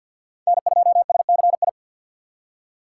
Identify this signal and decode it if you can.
Morse code, heard as a keyed tone, about 62 wpm, 700 Hz. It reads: N2SCI
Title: Speed Racing - USA 1x3 Call Signs 50wpm